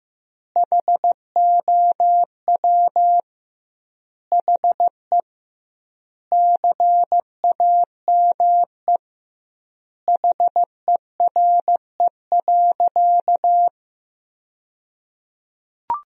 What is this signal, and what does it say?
Morse code, heard as a keyed tone, about 15 wpm, 700 Hz. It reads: HOW HE CAME HERE.
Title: 2x Word-Spacing - Sentences from Top 500 Words - 15wpm